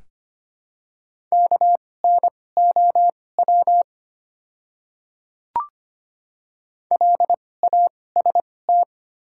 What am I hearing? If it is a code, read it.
Morse code, heard as a keyed tone, about 25 wpm, spaced as for 18 wpm, 700 Hz. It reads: KNOW LAST
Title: 2x Character Spacing - Top 300 Words - 25wpm